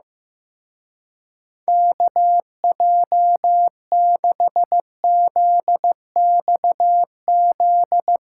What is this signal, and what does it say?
Morse code, heard as a keyed tone, about 15 wpm, 700 Hz. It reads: KJ6ZXZ
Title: Speed Racing - USA 2x3 Call Signs 15wpm